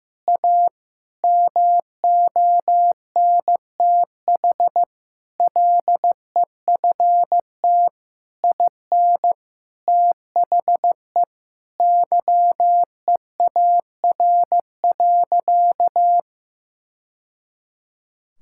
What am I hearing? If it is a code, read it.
Morse code, heard as a keyed tone, about 16 wpm, 700 Hz. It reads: A MONTH LEFT IN THE YEAR.
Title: Sentences from Top 1000 Words - Review Part 5 - 15wpm